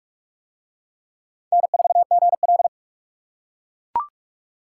Morse code, heard as a keyed tone, about 38 wpm, 700 Hz. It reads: N4GL